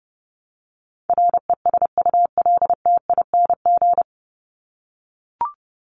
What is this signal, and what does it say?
Morse code, heard as a keyed tone, about 30 wpm, 700 Hz. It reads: RESULTING
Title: ING Suffix - Top 100 Words with Base Words 30wpm